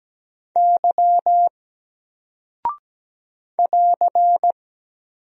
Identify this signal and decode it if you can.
Morse code, heard as a keyed tone, about 17 wpm, 700 Hz. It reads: Y <AR>
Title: Single Character (Letter:Number:Punctuation:Prosign) Alt 17wpm